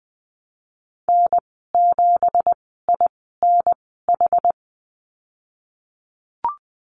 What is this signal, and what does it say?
Morse code, heard as a keyed tone, about 20 wpm, spaced as for 15 wpm, 700 Hz. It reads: N7INH
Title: 186 Lesson 30g - Letter b - Call Signs 20wpm